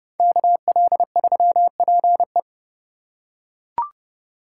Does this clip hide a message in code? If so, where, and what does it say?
Yes: Morse code from 0.2 to 2.4 s, heard as a keyed tone, about 28 wpm, 700 Hz. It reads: KL3PE